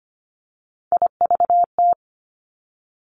Morse code, heard as a keyed tone, about 25 wpm, 700 Hz. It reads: IVT
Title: Three Challenging Letter-Number Combinations 25wpm